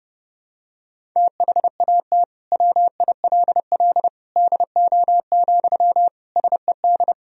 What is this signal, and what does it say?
Morse code, heard as a keyed tone, about 31 wpm, 700 Hz. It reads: THAT WILL DO, SED
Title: Sentences from Top 100 Words and Abbreviations 30wpm